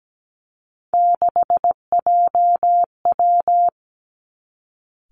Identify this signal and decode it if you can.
Morse code, heard as a keyed tone, about 17 wpm, 700 Hz. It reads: 6JW
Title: Three Challenging Letter-Number Combinations 17wpm